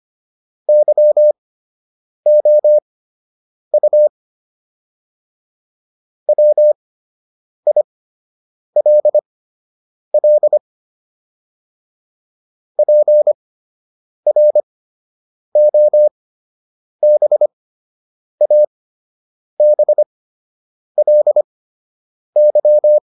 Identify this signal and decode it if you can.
Morse code, heard as a keyed tone, about 25 wpm, spaced as for 8 wpm, 600 Hz. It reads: YOU WILL PROBABLY